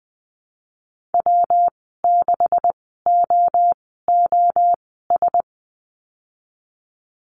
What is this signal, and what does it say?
Morse code, heard as a keyed tone, about 19 wpm, spaced as for 14 wpm, 700 Hz. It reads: W6OOS
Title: 179 Lesson 29g - Slash - Call Signs 20wpm